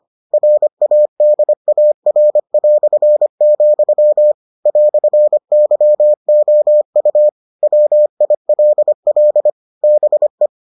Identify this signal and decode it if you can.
Morse code, heard as a keyed tone, about 25 wpm, 600 Hz. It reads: RADAR", "YOU WILL BE